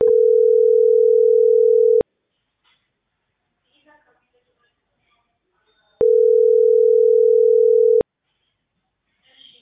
Ringback tone (North America), 440 + 480 Hz.